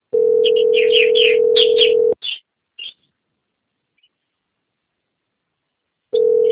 Ringback tone (North America), 440 + 480 Hz.